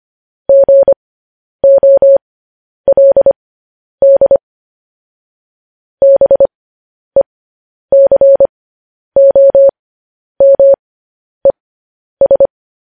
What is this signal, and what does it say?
Morse code, heard as a keyed tone, about 25 wpm, spaced as for 10 wpm, 550 Hz. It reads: GOLD BECOMES